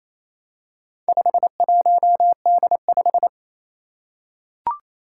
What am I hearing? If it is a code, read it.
Morse code, heard as a keyed tone, about 27 wpm, 700 Hz. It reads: 51D5
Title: Four Challenging Letter-Number Combinations 28wpm